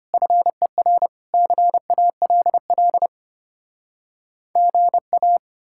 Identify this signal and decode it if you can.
Morse code, heard as a keyed tone, about 33 wpm, 700 Hz. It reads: FER CALL GA